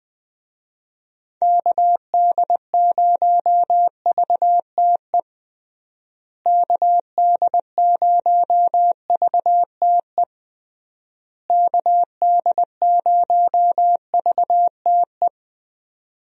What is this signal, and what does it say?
Morse code, heard as a keyed tone, about 20 wpm, 700 Hz. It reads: KD0VTE KD0VTE KD0VTE
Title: USA Call Signs - 6 Characters - Repeated 3x - 20wpm